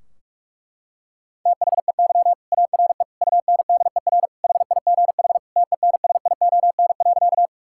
Morse code, heard as a keyed tone, about 42 wpm, 700 Hz. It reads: THEY ARE UNDER HIGH TENSION.
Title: Sentences from Top 1600 Words 45wpm